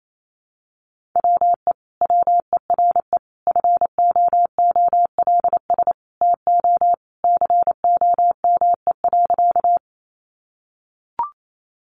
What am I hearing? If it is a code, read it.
Morse code, heard as a keyed tone, about 28 wpm, 700 Hz. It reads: WE WERE FOOLS TO COME.
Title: A Sound Of Thunder 28wpm